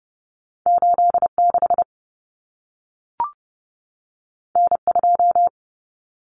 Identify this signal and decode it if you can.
Morse code, heard as a keyed tone, about 30 wpm, 700 Hz. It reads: 86 N2
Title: Common Cut Numbers - 2-Digit 30wpm